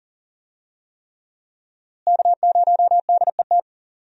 Morse code, heard as a keyed tone, about 40 wpm, 700 Hz. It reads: K0DET
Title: USA Call Signs - 5 Characters - Repeated 3x - 40wpm